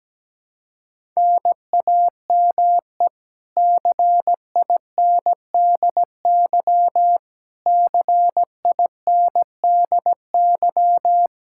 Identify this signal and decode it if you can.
Morse code, heard as a keyed tone, about 17 wpm, 700 Hz. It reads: NAME CINDY CINDY
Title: K1USN SST QSO Practice 17wpm